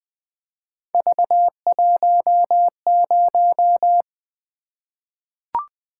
Morse code, heard as a keyed tone, about 20 wpm, 700 Hz. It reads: V10
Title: Three Letter-Number Combinations - 20wpm